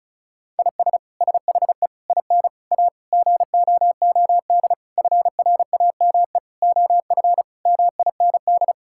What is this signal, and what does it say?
Morse code, heard as a keyed tone, about 36 wpm, 700 Hz. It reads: IS SHE IN A GOOD FRAME OF MIND
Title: Sentences from Top 1000 Words - Review Part 6 - 35wpm